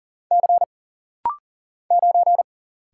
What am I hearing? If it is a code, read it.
Morse code, heard as a keyed tone, about 40 wpm, 700 Hz. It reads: C 9
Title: Single Letter-Number - Rapid-Fire 40wpm